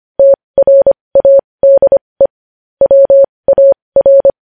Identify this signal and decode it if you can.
Morse code, heard as a keyed tone, about 25 wpm, spaced as for 20 wpm, 550 Hz. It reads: TRADE WAR
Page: news headlines in morse code at 20 wpm